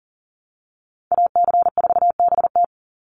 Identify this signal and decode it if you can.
Morse code, heard as a keyed tone, about 40 wpm, 700 Hz. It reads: AC4BT
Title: Call Signs Repeated 3x - 40wpm